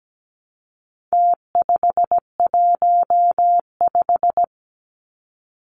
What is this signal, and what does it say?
Morse code, heard as a keyed tone, about 17 wpm, 700 Hz. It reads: T515
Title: Numbers - Contest Style - 4-Digit 17wpm